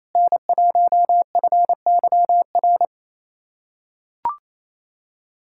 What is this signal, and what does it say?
Morse code, heard as a keyed tone, about 27 wpm, 700 Hz. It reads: N1FYR
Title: USA Callsign DE Callsign 28wpm